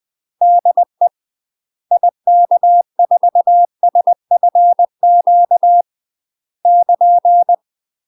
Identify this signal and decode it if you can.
Morse code, heard as a keyed tone, about 20 wpm, 700 Hz. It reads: DE IK4SFQ <KN>
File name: UK MASTER QSOs TRAINING/20WPM